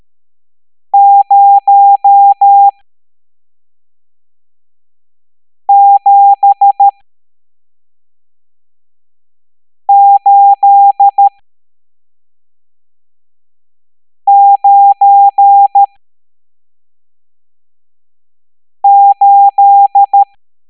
Morse code, heard as a keyed tone, about 13 wpm, spaced as for 3 wpm, 800 Hz. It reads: 07898
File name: hamdata / L11-EX01-67890